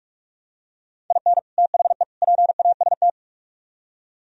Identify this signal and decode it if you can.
Morse code, heard as a keyed tone, about 44 wpm, 700 Hz. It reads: IN THE PAST